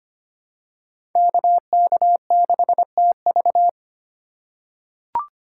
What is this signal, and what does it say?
Morse code, heard as a keyed tone, about 25 wpm, 700 Hz. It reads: KK6TV
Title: K1USN SST QSO Practice - 25wpm